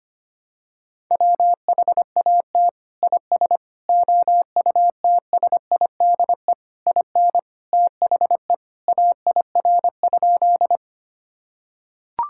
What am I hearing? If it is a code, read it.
Morse code, heard as a keyed tone, about 25 wpm, 700 Hz. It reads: WHAT IS OUTSIDE IN THE AIR?
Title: Sentences from Top 1000 Words - Review Part 6 - 25wpm